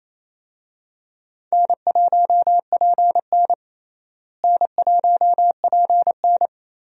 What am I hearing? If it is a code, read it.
Morse code, heard as a keyed tone, about 27 wpm, 700 Hz. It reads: N1PN N1PN
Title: USA Call Signs - 4 Characters - Repeated 2x - 28wpm